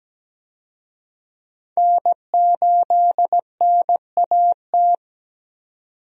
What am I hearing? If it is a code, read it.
Morse code, heard as a keyed tone, about 17 wpm, 700 Hz. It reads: N8NAT
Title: USA Call Signs - 5 Characters - 17wpm